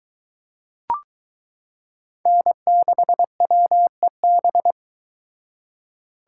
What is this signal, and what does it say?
Morse code, heard as a keyed tone, about 23 wpm, 700 Hz. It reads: N6WEB